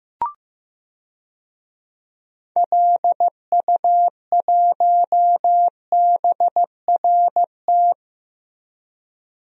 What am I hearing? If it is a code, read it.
Morse code, heard as a keyed tone, about 15 wpm, 700 Hz. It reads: LU1BRT
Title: DXCC-100 Call Signs - 6 Characters - Repeated 3x - 15wpm